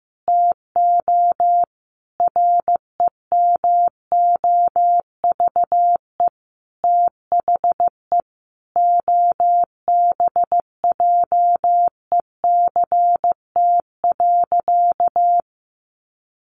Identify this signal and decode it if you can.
Morse code, heard as a keyed tone, about 15 wpm, 700 Hz. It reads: TO REMOVE THE OBJECT.
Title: Sentences from Top 800 Words 15wpm